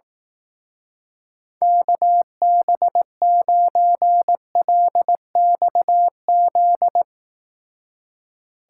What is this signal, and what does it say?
Morse code, heard as a keyed tone, about 18 wpm, 700 Hz. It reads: KB9LXZ
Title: Speed Racing - USA Call Signs 15wpm